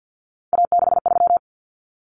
Morse code, heard as a keyed tone, about 50 wpm, 700 Hz. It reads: A63
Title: Common Cut Numbers - 3-Digit 50wpm